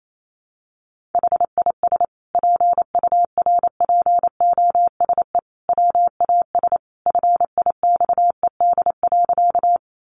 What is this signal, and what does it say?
Morse code, heard as a keyed tone, about 28 wpm, 700 Hz. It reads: HIS PURPOSE WAS FIXED.
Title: Sentences from Top 1100 Words 28wpm